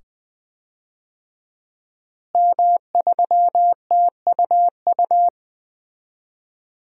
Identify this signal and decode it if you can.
Morse code, heard as a keyed tone, about 20 wpm, 700 Hz. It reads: M3TUU
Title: DXCC-100 Call Signs - 5 Characters - Repeated 2x - 20wpm